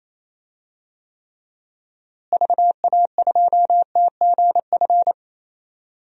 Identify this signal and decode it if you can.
Morse code, heard as a keyed tone, about 28 wpm, 700 Hz. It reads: VA2TGF
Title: DXCC-100 Call Signs - 6 Characters - Repeated 3x - 28wpm